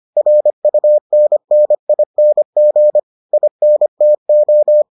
Morse code, heard as a keyed tone, about 25 wpm, 600 Hz. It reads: RUNNING INTO